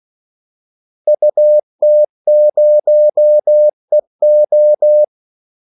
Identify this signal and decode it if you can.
Morse code, heard as a keyed tone, about 16 wpm, 600 Hz. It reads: UT0EO
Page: Kaldesignaler 16 wpm